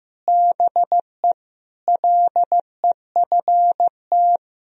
Morse code, heard as a keyed tone, about 15 wpm, 700 Hz. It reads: BE LEFT
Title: Sentences from Top 1000 Words - Review Part 4 - 15wpm